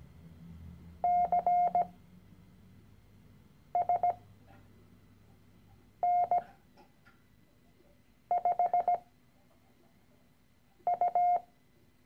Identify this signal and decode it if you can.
Morse code, heard as a keyed tone, about 17 wpm, spaced as for 4 wpm, 710 Hz. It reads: CSN5U
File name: CW-Online-Kurs / Anfänger / Lektion 013